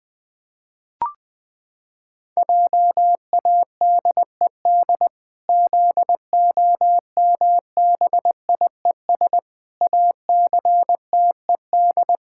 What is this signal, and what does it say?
Morse code, heard as a keyed tone, about 21 wpm, 700 Hz. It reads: JADED ZOMBIES ACTED
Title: Pangram Challenge v2 - 20wpm